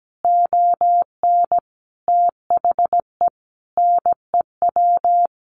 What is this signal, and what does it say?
Morse code, heard as a keyed tone, about 17 wpm, 700 Hz. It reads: ON THE NEW
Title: A Sound Of Thunder 17wpm